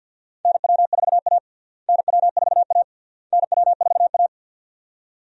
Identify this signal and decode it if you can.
Morse code, heard as a keyed tone, about 46 wpm, 700 Hz. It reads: NW4A NW4A NW4A
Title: USA Call Signs - 4 Characters - Repeated 3x - 50wpm